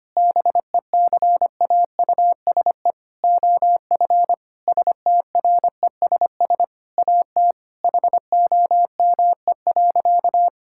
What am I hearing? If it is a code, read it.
Morse code, heard as a keyed tone, about 25 wpm, 700 Hz. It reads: BECAUSE OF STRESS AT HOME.
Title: Sentences from Top 1000 Words - Review Part 6 - 25wpm